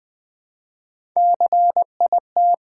Morse code, heard as a keyed tone, about 20 wpm, 700 Hz. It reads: CIT
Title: Build A Word - 1 Letter At A Time - 4-Letter Words - 20wpm